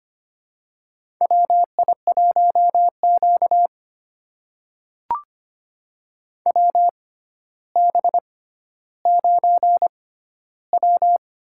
Morse code, heard as a keyed tone, about 25 wpm, spaced as for 13 wpm, 700 Hz. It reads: WI1Q WB9W
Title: USA Call Signs - 6x Character Spacing - 25wpm